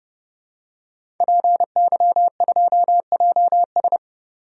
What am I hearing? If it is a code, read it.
Morse code, heard as a keyed tone, about 32 wpm, 700 Hz. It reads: PY2JS
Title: DXCC-100 Call Signs - 5 Characters - 30wpm